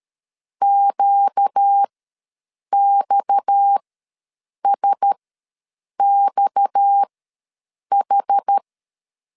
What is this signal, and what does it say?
Morse code, heard as a keyed tone, about 13 wpm, spaced as for 7 wpm, 800 Hz. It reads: QXSXH
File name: LEZIONI / 6 PKX0 / 6 PKX ws 12wpm 3106